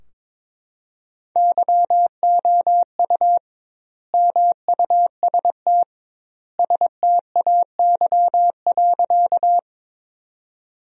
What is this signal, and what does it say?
Morse code, heard as a keyed tone, about 22 wpm, 700 Hz. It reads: YOU MUST STAY.
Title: 2x Word-Spacing - Sentences from Top 500 Words - 22wpm